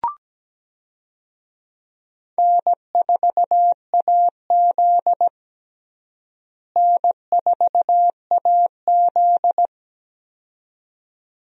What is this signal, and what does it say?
Morse code, heard as a keyed tone, about 17 wpm, 700 Hz. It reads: N4AZ N4AZ
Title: USA Call Signs - 4 Characters - Repeated 2x - 17wpm